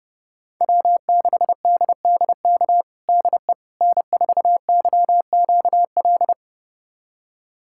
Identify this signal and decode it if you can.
Morse code, heard as a keyed tone, about 30 wpm, 700 Hz. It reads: W6DDK DE N4YQL
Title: USA Callsign DE Callsign - 5 Characters - 30wpm